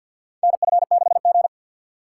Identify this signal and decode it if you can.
Morse code, heard as a keyed tone, about 49 wpm, 690 Hz. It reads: NF6G
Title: USA Call Signs - 4x Character Spacing - 50wpm